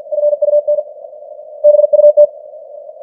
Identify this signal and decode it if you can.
Morse code, heard as a keyed tone, about 50 wpm, 590 Hz. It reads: FUN BUT
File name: archived / CWBros